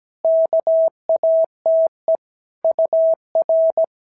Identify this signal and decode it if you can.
Morse code, heard as a keyed tone, about 17 wpm, 650 Hz. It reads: KATE UR
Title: SOTA QSO Practice 17wpm